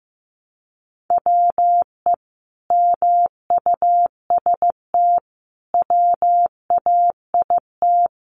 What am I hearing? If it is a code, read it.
Morse code, heard as a keyed tone, about 15 wpm, 700 Hz. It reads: WE MUST WAIT
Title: Sentences from Top 1300 Words 15wpm